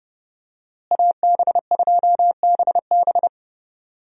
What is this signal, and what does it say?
Morse code, heard as a keyed tone, about 31 wpm, 700 Hz. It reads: AB2BB
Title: USA Call Signs - 5 Characters - Repeated 3x - 30wpm